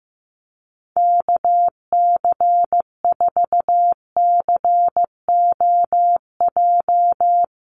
Morse code, heard as a keyed tone, about 15 wpm, 700 Hz. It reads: KC4COJ
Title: USA Call Signs Repeated 2x 15wpm